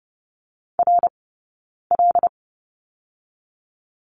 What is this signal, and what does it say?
Morse code, heard as a keyed tone, about 30 wpm, 700 Hz. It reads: R L
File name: Lesson 18d - Letter m - Two Characters 30wpm